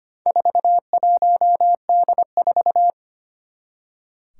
Morse code, heard as a keyed tone, about 25 wpm, 700 Hz. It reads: 41D4
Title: Four Challenging Letter-Number Combinations 25wpm